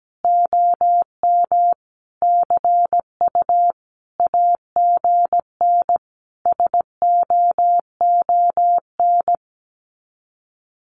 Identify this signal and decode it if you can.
Morse code, heard as a keyed tone, about 17 wpm, 700 Hz. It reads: OM CU AGN SOON
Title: Common QSO Phrases 17wpm